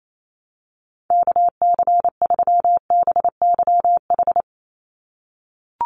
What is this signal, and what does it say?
Morse code, heard as a keyed tone, about 28 wpm, 700 Hz. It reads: KC3BYH
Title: USA Call Signs 28wpm